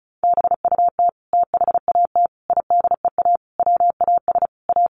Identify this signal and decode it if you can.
Morse code, heard as a keyed tone, about 34 wpm, 700 Hz. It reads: BUT THAT IDEA WAS A
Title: Sentences from Top 900 Words 35wpm